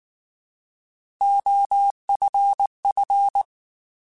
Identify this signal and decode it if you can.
Morse code, heard as a keyed tone, about 19 wpm, 800 Hz. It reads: OFF
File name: Alphabetical / Alphabetical mp3s / Os @18